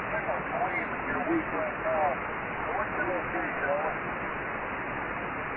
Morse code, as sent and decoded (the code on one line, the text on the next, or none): none